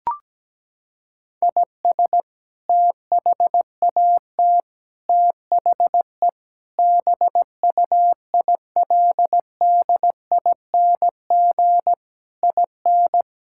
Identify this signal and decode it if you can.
Morse code, heard as a keyed tone, about 17 wpm, 700 Hz. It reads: IS THAT THE BUILDING IN
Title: Sentences from Top 1000 Words - Review Part 5 - 17wpm